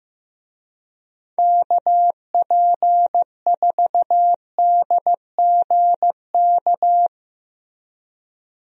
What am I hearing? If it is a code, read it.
Morse code, heard as a keyed tone, about 16 wpm, 700 Hz. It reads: KP4DGK
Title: USA Call Signs - 6 Characters - Repeated 2x - 15wpm